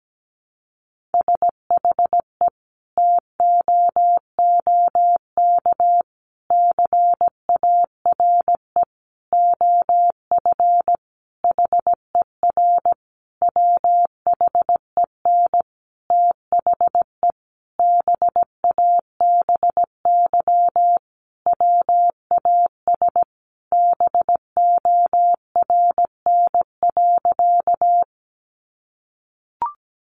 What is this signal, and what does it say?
Morse code, heard as a keyed tone, about 17 wpm, 700 Hz. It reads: SHE TOOK CARE OF HER WHEN THE BABY WAS BORN.
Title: Sentences from Top 1300 Words 17wpm